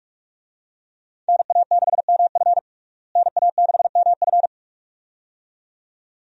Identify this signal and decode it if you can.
Morse code, heard as a keyed tone, about 43 wpm, 700 Hz. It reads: NA6MF NA6MF
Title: USA Call Signs - 5 Characters - Repeated 2x - 45wpm